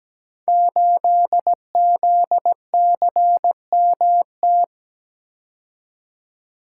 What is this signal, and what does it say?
Morse code, heard as a keyed tone, about 17 wpm, 700 Hz. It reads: 8ZCMT
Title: Code Groups - 5 Letter:Numbers - 17wpm